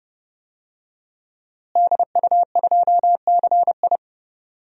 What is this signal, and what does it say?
Morse code, heard as a keyed tone, about 30 wpm, 700 Hz. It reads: DU2CI
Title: DXCC-100 Call Signs - 5 Characters - Repeated 3x - 30wpm